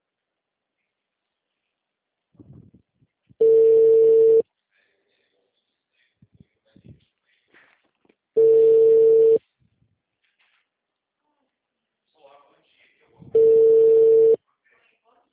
Ringback tone (China, 450 Hz).